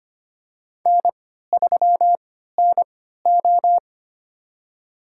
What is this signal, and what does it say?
Morse code, heard as a keyed tone, about 25 wpm, spaced as for 14 wpm, 700 Hz. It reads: N3NO
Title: USA Call Signs - 3x Character Spacing - 25wpm